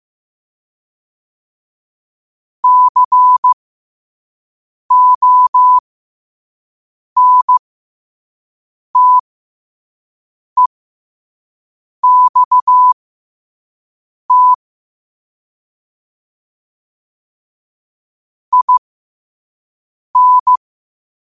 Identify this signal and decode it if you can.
Morse code, heard as a keyed tone, about 15 wpm, spaced as for 5 wpm, 1000 Hz. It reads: CONTEXT IN
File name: qotdfiles / 5wpm